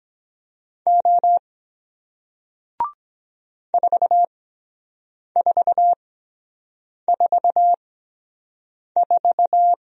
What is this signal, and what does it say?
Morse code, heard as a keyed tone, about 23 wpm, spaced as for 7 wpm, 700 Hz. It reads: O 4444